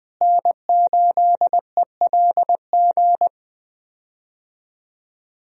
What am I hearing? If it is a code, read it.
Morse code, heard as a keyed tone, about 21 wpm, 700 Hz. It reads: N8ELG
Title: USA Call Signs - 5 Characters - Repeated 3x - 20wpm